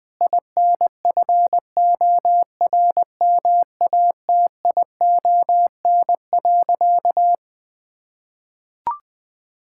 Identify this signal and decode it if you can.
Morse code, heard as a keyed tone, about 20 wpm, 700 Hz. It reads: INFORMATION.
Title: Sentences from Top 1000 Words - Review Part 6 - 20wpm